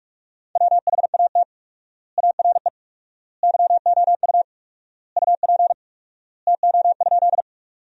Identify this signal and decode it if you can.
Morse code, heard as a keyed tone, about 42 wpm, 700 Hz. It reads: WHAT ARE YOU UP TO?
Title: 4x Word-Spacing - Sentences from Top 100 Words - 45wpm